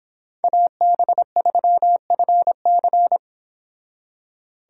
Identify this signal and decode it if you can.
Morse code, heard as a keyed tone, about 30 wpm, 700 Hz. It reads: AB3FC